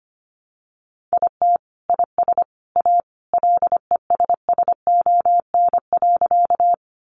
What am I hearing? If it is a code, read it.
Morse code, heard as a keyed tone, about 25 wpm, 700 Hz. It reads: IT IS A LESSON.
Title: Sentences from Top 1200 Words 25wpm